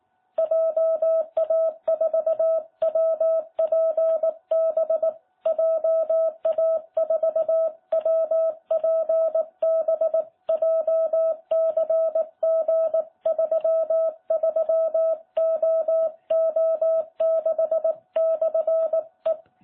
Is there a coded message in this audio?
Yes: Morse code from 0.4 to 19.3 s, heard as a keyed tone, about 18 wpm, 660 Hz. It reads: JA4WPB JA4WPB JCG33OO6/E